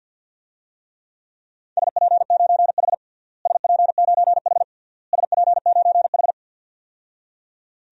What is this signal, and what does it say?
Morse code, heard as a keyed tone, about 51 wpm, 700 Hz. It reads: SP9H SP9H SP9H
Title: DXCC-100 Call Signs - 4 Characters - Repeated 3x - 50wpm